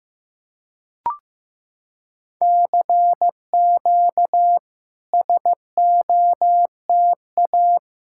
Morse code, heard as a keyed tone, about 15 wpm, 700 Hz. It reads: CQ SOTA
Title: SOTA QSO Practice 15wpm